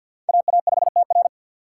Morse code, heard as a keyed tone, about 53 wpm, 700 Hz. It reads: AA5TR
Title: USA Call Signs - 5 Characters - 6x Character Spacing - 50wpm